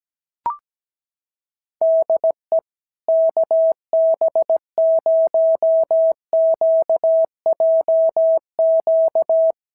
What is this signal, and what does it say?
Morse code, heard as a keyed tone, about 17 wpm, 660 Hz. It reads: DE KB0QJQ
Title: SOTA QSO Practice 17wpm